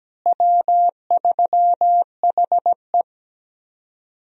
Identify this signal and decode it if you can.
Morse code, heard as a keyed tone, about 17 wpm, 700 Hz. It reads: W3HE
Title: USA Call Signs - 4 Characters - Repeated 3x - 17wpm